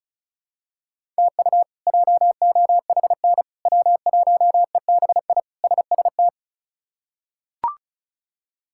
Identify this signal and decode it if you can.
Morse code, heard as a keyed tone, about 34 wpm, 700 Hz. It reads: TU JOHN W1EBI SST
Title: K1USN SST QSO Practice - 35wpm